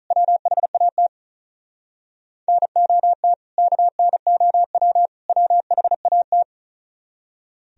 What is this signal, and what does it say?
Morse code, heard as a keyed tone, about 44 wpm, 690 Hz. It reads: WHAT NOT KNOW WHAT